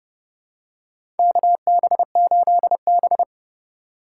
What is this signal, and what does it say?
Morse code, heard as a keyed tone, about 31 wpm, 700 Hz. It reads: KB8B
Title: USA Call Signs - 4 Characters - 4x Character Spacing - 30wpm